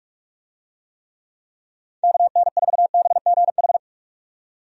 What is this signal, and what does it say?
Morse code, heard as a keyed tone, about 46 wpm, 700 Hz. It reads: KN4BGH
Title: USA Call Signs - 6 Characters - 45wpm